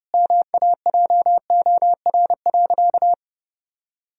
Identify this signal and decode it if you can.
Morse code, heard as a keyed tone, about 30 wpm, 700 Hz. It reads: MAJOR.
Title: Sentences from Top 1000 Words - Review Part 4 - 30wpm